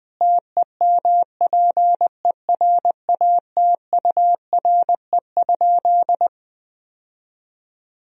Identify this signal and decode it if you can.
Morse code, heard as a keyed tone, about 20 wpm, 700 Hz. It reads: TEMPERATURE?
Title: Sentences from Top 1000 Words - Review Part 4 - 20wpm